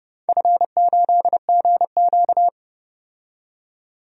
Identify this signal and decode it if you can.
Morse code, heard as a keyed tone, about 30 wpm, 700 Hz. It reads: F8GQ
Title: DXCC-100 Call Signs - 4 Characters - 30wpm